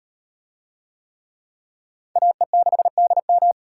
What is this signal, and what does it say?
Morse code, heard as a keyed tone, about 36 wpm, 700 Hz. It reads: AE6DM